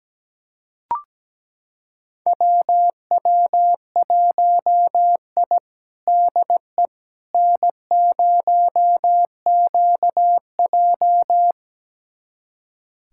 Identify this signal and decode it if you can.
Morse code, heard as a keyed tone, about 17 wpm, 700 Hz. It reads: WW1I DE N0QJ
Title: USA Callsign DE Callsign - 4 Characters - 17wpm